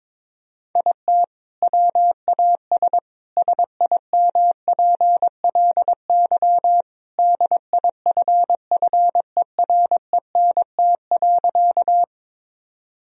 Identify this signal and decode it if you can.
Morse code, heard as a keyed tone, about 22 wpm, 700 Hz. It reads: IT WAS SIMPLY DIFFERENT.
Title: Sentences from Top 1000 Words - Review Part 4 - 22wpm